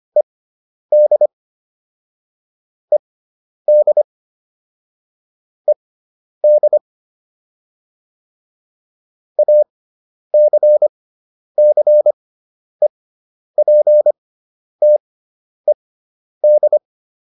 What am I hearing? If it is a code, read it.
Morse code, heard as a keyed tone, about 25 wpm, spaced as for 10 wpm, 600 Hz. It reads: ED ED ED ACCEPTED